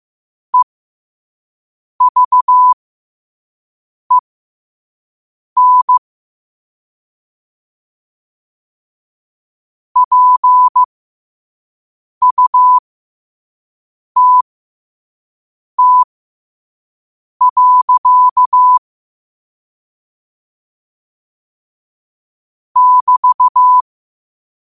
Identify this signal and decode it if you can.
Morse code, heard as a keyed tone, about 15 wpm, spaced as for 5 wpm, 1000 Hz. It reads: EVEN PUTT. <BT>
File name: qotdfiles / 5wpm